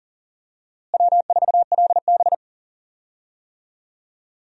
Morse code, heard as a keyed tone, about 40 wpm, 700 Hz. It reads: W4LB
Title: USA Call Signs - 4 Characters - Repeated 2x - 40wpm